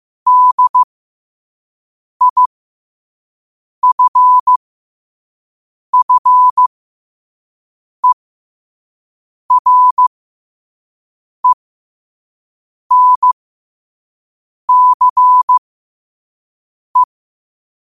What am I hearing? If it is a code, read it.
Morse code, heard as a keyed tone, about 15 wpm, spaced as for 5 wpm, 1000 Hz. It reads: DIFFERENCE